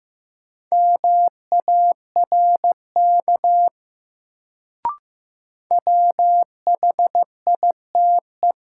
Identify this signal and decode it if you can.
Morse code, heard as a keyed tone, about 15 wpm, 700 Hz. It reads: MARK WHITE
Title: Sentences from Top 300 Words - New Word Drill - 15wpm